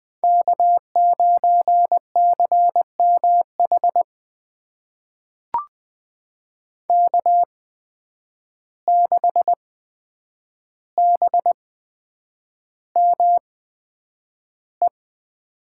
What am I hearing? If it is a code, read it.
Morse code, heard as a keyed tone, about 20 wpm, spaced as for 9 wpm, 700 Hz. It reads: K9CMH K6BME
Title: USA Call Signs - 8x Character Spacing - 20wpm